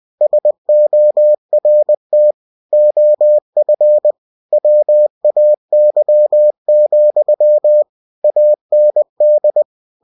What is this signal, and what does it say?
Morse code, heard as a keyed tone, about 20 wpm, 600 Hz. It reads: SORT OF WAY, AND
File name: cw-books / winnie-the-pooh / winnie-20wpm-page-0022